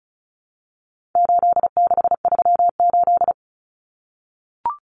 Morse code, heard as a keyed tone, about 34 wpm, 700 Hz. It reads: 8638
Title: Common Cut Numbers - 4-Digit 35wpm